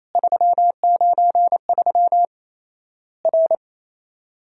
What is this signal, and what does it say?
Morse code, heard as a keyed tone, about 28 wpm, 700 Hz. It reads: 393 R
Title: Worked All Europe DX Contest - QTC Exchange - 28wpm